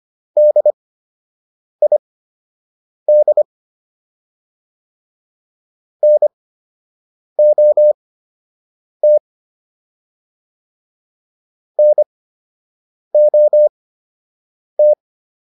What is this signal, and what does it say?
Morse code, heard as a keyed tone, about 25 wpm, spaced as for 7 wpm, 600 Hz. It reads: DID NOT NOT